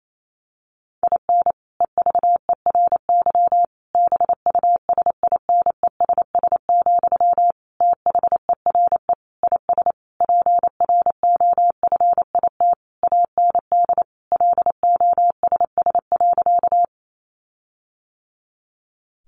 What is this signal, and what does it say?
Morse code, heard as a keyed tone, about 28 wpm, 700 Hz. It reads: IN EVERY BUSINESS, THERE IS PROFIT AND LOSS.
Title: Sentences from Top 1400 Words 28wpm